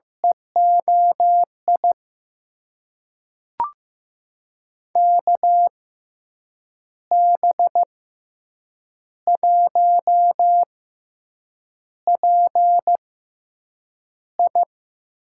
Morse code, heard as a keyed tone, about 16 wpm, spaced as for 7 wpm, 700 Hz. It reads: EOI KB1PI
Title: USA Call Signs - 6 Characters - 6x Character Spacing - 15wpm